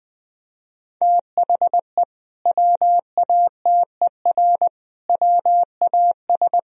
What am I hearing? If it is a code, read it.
Morse code, heard as a keyed tone, about 20 wpm, 700 Hz. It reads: THE WATER WAS
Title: Sentences from Top 1000 Words - Review Part 3 - 20wpm